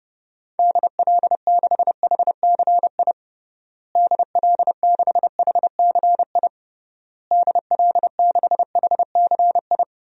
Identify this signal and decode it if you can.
Morse code, heard as a keyed tone, about 30 wpm, 700 Hz. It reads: DL6HCI DL6HCI DL6HCI
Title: DXCC-100 Call Signs - 6 Characters - Repeated 3x - 30wpm